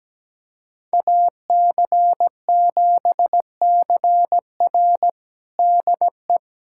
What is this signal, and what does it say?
Morse code, heard as a keyed tone, about 17 wpm, 700 Hz. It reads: AC7CR DE
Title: USA Callsign DE Callsign - 5 Characters - 17wpm